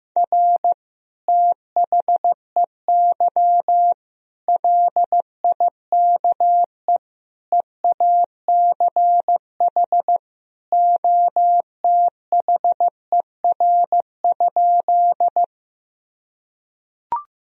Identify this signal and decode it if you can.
Morse code, heard as a keyed tone, about 15 wpm, 700 Hz. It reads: R THEY LIKE EACH OTHER?
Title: Sentences from Top 200 Words and Abbreviations 15wpm